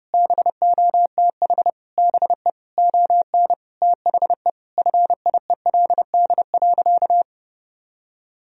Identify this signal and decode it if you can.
Morse code, heard as a keyed tone, about 32 wpm, 700 Hz. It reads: BOTH BE ON THE FIELD.
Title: Sentences from Top 1000 Words - Review Part 2 - 30wpm